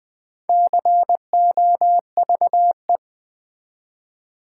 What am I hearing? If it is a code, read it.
Morse code, heard as a keyed tone, about 20 wpm, 700 Hz. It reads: COVE
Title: Build A Word - 1 Letter At A Time - 5-Letter Words - 20wpm